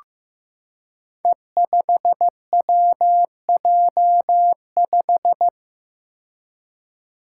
Morse code, heard as a keyed tone, about 15 wpm, 700 Hz. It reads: E5WJ5
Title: Five Challenging Letter-Number Combinations 15wpm